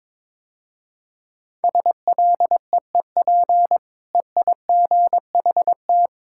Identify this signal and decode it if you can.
Morse code, heard as a keyed tone, about 22 wpm, 700 Hz. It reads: SLEEP EIGHT
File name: practice / 22wpm / Easy Sentences with Hints - 22wpm